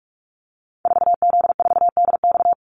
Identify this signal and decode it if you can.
Morse code, heard as a keyed tone, about 46 wpm, 700 Hz. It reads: 4Z4DX
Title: Call Signs - 45wpm